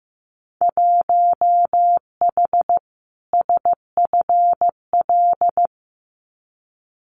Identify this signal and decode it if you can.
Morse code, heard as a keyed tone, about 15 wpm, 700 Hz. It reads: 1H SFL
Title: Winter Field Day Exchanges 15wpm